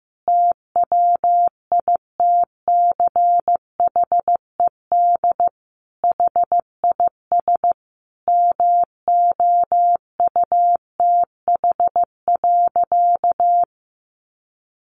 Morse code, heard as a keyed tone, about 15 wpm, 700 Hz. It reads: TWITCHED HIS MOUTH.
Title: A Sound Of Thunder 15wpm